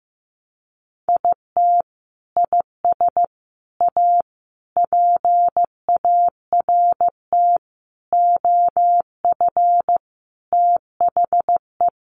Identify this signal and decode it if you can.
Morse code, heard as a keyed tone, about 15 wpm, 700 Hz. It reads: IT IS A PART OF THE
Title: Sentences from Top 1400 Words 15wpm